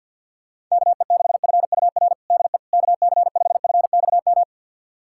Morse code, heard as a keyed tone, about 50 wpm, 700 Hz. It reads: KE6FUR DE KC5FXM
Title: USA Callsign DE Callsign 50wpm